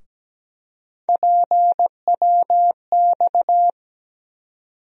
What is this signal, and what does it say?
Morse code, heard as a keyed tone, about 17 wpm, 700 Hz. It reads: PWX
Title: USA Call Sign Postfixes - 3 Letters 17wpm